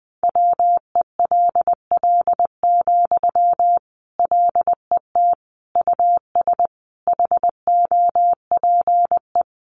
Morse code, heard as a keyed tone, about 20 wpm, 700 Hz. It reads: WELL, LET US HOPE
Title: Sentences from Top 1000 Words 20wpm